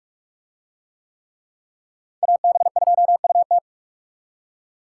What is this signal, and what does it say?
Morse code, heard as a keyed tone, about 41 wpm, 700 Hz. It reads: AB2UT